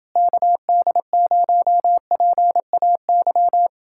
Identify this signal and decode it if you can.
Morse code, heard as a keyed tone, about 29 wpm, 700 Hz. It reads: KD0PAY